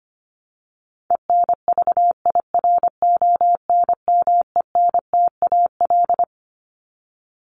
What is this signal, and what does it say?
Morse code, heard as a keyed tone, about 25 wpm, 700 Hz. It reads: ENVIRONMENTAL
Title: Top 1001-1250 Words 25wpm